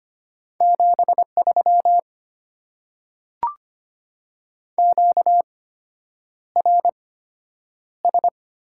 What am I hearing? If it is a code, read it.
Morse code, heard as a keyed tone, about 25 wpm, spaced as for 9 wpm, 690 Hz. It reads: 73 QRS
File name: Top 90 QSO Elements - 8x Character Spacing - 25wpm